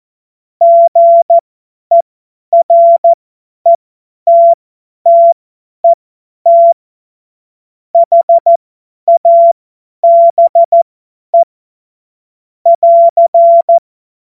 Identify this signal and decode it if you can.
Morse code, heard as a keyed tone, about 14 wpm, spaced as for 10 wpm, 680 Hz. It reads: GERETTET HABE <AR>
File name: Witz-01-Juli-2025-50-F70